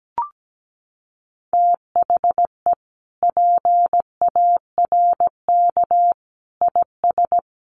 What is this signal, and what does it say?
Morse code, heard as a keyed tone, about 17 wpm, 700 Hz. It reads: THE PARK IS
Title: Sentences from Top 1300 Words 17wpm